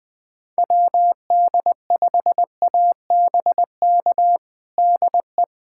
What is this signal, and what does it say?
Morse code, heard as a keyed tone, about 21 wpm, 700 Hz. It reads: WD5ABK DE
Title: USA Callsign DE Callsign 20wpm